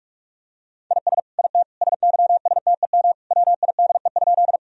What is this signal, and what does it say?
Morse code, heard as a keyed tone, about 47 wpm, 700 Hz. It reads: IS IT SYSTEM WIDE?
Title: Sentences from Top 1000 Words - Review Part 3 - 45wpm